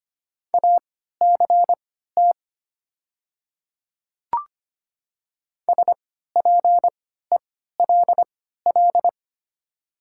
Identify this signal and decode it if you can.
Morse code, heard as a keyed tone, about 25 wpm, spaced as for 14 wpm, 700 Hz. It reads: ACT SPELL
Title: 3x Character Spacing - Top 200 Words - 25wpm